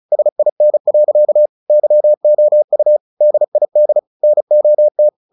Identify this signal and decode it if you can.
Morse code, heard as a keyed tone, about 37 wpm, 600 Hz. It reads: SIN. YOU DID NOT
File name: the old man and the sea 2 35wpm